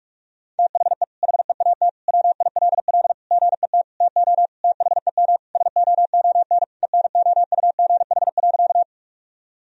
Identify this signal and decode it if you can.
Morse code, heard as a keyed tone, about 48 wpm, 700 Hz. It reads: THE HEAT WILL GET TO THEM SOON ENOUGH.
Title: Sentences from Top 1000 Words - Review Part 3 - 45wpm